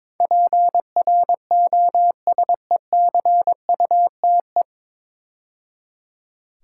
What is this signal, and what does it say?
Morse code, heard as a keyed tone, about 22 wpm, 700 Hz. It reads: PROSECUTE
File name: 9 Letter Words - First 3 Letters Given - 22wpm